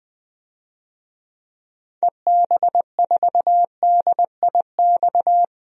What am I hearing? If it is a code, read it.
Morse code, heard as a keyed tone, about 20 wpm, 700 Hz. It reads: EB4DIX
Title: DXCC-100 Call Signs - 6 Characters - 20wpm